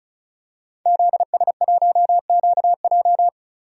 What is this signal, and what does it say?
Morse code, heard as a keyed tone, about 36 wpm, 700 Hz. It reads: ZS1QJ
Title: DXCC-100 Call Signs - 5 Characters - 35wpm